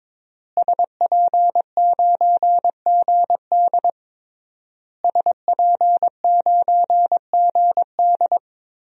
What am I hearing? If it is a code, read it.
Morse code, heard as a keyed tone, about 22 wpm, 700 Hz. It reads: SP9GD SP9GD
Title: DXCC-100 Call Signs - 5 Characters - Repeated 3x - 22wpm